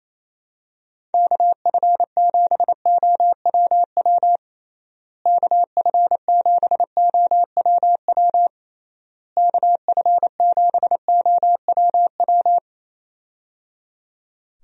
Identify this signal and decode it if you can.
Morse code, heard as a keyed tone, about 27 wpm, 700 Hz. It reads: KF7OWW KF7OWW KF7OWW
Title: USA Call Signs - 6 Characters - Repeated 3x - 28wpm